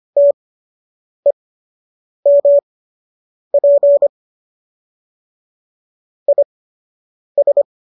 Morse code, heard as a keyed tone, about 25 wpm, spaced as for 8 wpm, 570 Hz. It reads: TEMP IS
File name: qso006 08